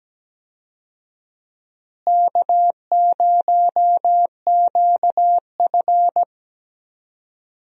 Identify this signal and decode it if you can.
Morse code, heard as a keyed tone, about 17 wpm, 700 Hz. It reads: K0QF
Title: USA Call Signs - 4 Characters - Repeated 2x - 17wpm